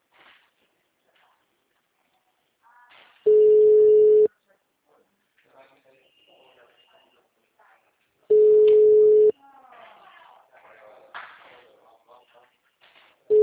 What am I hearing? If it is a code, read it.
Ringback tone (Europe, 425 Hz), with speech.